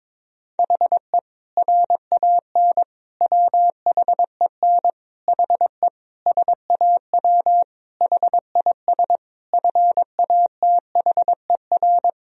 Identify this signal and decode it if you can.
Morse code, heard as a keyed tone, about 23 wpm, 700 Hz. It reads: HE RAN WHEN HE SAW HIS FATHER
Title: Sentences from Top 500 Words and Abbreviations 22wpm